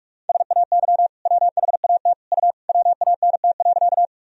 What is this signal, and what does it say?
Morse code, heard as a keyed tone, about 42 wpm, 700 Hz. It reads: SAY WHAT U WANT.
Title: Sentences from Top 200 Words and Abbreviations 45wpm